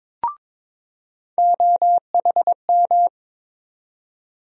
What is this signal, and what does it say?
Morse code, heard as a keyed tone, about 21 wpm, 700 Hz. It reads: OHM
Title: USA Call Sign Postfixes - 3 Letters 22wpm